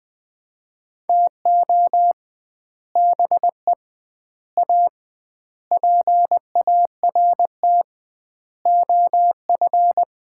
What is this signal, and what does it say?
Morse code, heard as a keyed tone, about 20 wpm, 700 Hz. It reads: TO BE A PART OF
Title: 2x Word-Spacing - Sets of 5 Words - 20wpm